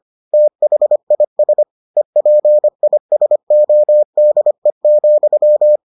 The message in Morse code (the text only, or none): THIS EPISODE,